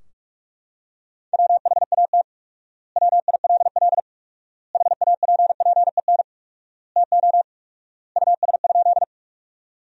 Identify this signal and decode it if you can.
Morse code, heard as a keyed tone, about 43 wpm, 700 Hz. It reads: WHAT WILL HAPPEN TO US?
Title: 4x Word-Spacing - Sentences from Top 400 Words - 45wpm